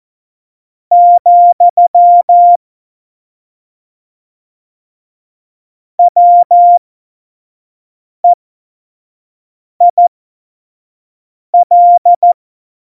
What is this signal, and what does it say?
Morse code, heard as a keyed tone, about 14 wpm, spaced as for 5 wpm, 700 Hz. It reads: , WEIL
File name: Morsekurs / Witze / Witz 04. Dezember 2018, 25 F70